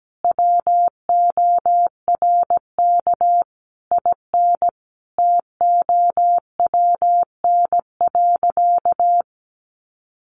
Morse code, heard as a keyed tone, about 17 wpm, 700 Hz. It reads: WORK IN TOWN.
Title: Sentences from Top 1500 Words 17wpm